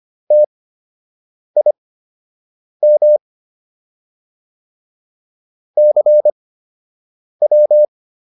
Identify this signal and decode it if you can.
Morse code, heard as a keyed tone, about 25 wpm, spaced as for 7 wpm, 600 Hz. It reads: TIM CW